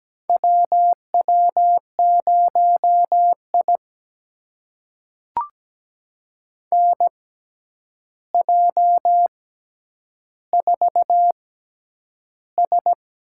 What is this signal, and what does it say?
Morse code, heard as a keyed tone, about 17 wpm, spaced as for 9 wpm, 700 Hz. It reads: WW0I NJ4S
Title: USA Call Signs - 4 Characters - 6x Character Spacing - 17wpm